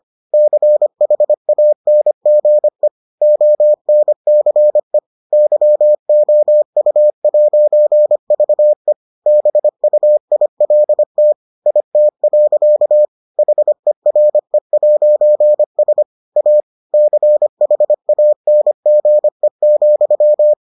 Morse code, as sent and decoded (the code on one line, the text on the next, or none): -.-. .... .- -. --. / --- -. -.-. / -.-- --- ..- .----. ...- . / -... ..- .. .-.. - / .. - .-.-.- / .... .-. .----. ... / .- / -.-. .... .- -. --. --..--
CHANGE ONCE YOU'VE BUILT IT. HERE'S A CHANGE,